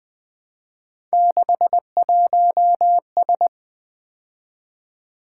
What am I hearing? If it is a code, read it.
Morse code, heard as a keyed tone, about 23 wpm, 700 Hz. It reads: 61S
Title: Speed-Racing - Three Letter-Number Combinations - 20wpm